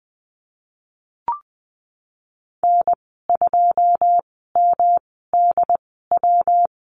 Morse code, heard as a keyed tone, about 20 wpm, spaced as for 15 wpm, 700 Hz. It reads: N2MDW
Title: 154 Lesson 25g - Letter p - Call Signs 20wpm